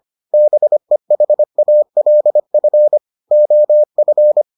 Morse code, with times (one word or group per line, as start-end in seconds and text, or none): 0.3-3.0 BEHALF
3.3-4.4 OF